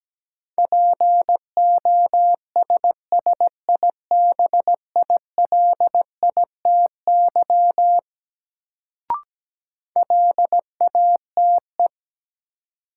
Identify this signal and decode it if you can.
Morse code, heard as a keyed tone, about 17 wpm, 700 Hz. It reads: POSSIBILITY LATE
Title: Sets of 3 Words - New Word Drill - 17wpm